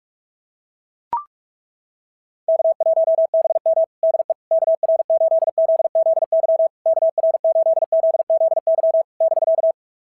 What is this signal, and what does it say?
Morse code, heard as a keyed tone, about 42 wpm, 650 Hz. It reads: K1BM DE KR8ZZY KR8ZZY <BK>
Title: SOTA QSO Practice 45wpm